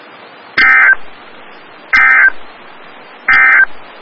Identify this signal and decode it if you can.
Emergency Alert System (SAME) tones, heard as harsh buzzing bursts. It is heard from 0.5 to 3.7 s.
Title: Des Moines, IA NOAA Weather Radio - WXL57